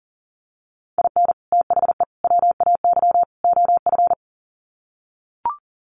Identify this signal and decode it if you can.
Morse code, heard as a keyed tone, about 40 wpm, 700 Hz. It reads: IN THE WAY OF
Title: Sets of 4 Words 40wpm